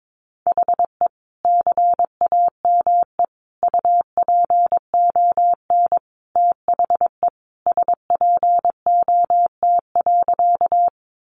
Morse code, heard as a keyed tone, about 22 wpm, 700 Hz. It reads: HE CAME UPON THE SPOT.
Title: Sentences from Top 1100 Words 22wpm